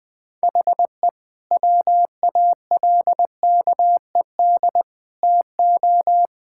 Morse code, heard as a keyed tone, about 20 wpm, 700 Hz. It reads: HE WALKED TO
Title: Sentences with 5-Word Phrases - Spoken Beginning - 20wpm